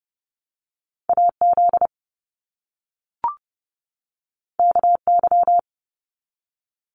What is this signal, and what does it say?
Morse code, heard as a keyed tone, about 30 wpm, 700 Hz. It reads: AZ KY
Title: US States 30wpm